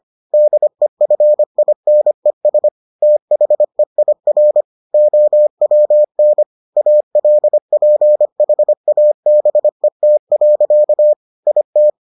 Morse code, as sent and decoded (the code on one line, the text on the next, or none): -.. ..-. .. -. ... / - .... .. .-. / --- .-- -. / .- .-.. .--. .... .- -... - .-.-.- / .. -
DEFINES THEIR OWN ALPHABET. IT